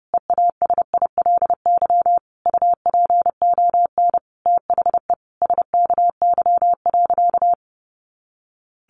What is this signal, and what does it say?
Morse code, heard as a keyed tone, about 30 wpm, 700 Hz. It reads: EASILY UPON THE SKY.
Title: A Sound Of Thunder 30wpm